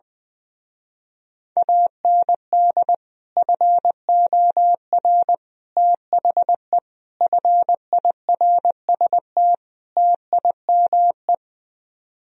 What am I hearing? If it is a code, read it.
Morse code, heard as a keyed tone, about 20 wpm, 700 Hz. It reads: AND FOR THE FIRST TIME
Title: Speed-Racing - Sets of 5 Words 20wpm